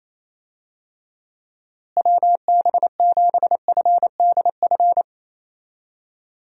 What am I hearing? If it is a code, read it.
Morse code, heard as a keyed tone, about 28 wpm, 700 Hz. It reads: WB7FDF
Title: USA Call Signs - 6 Characters - Repeated 2x - 28wpm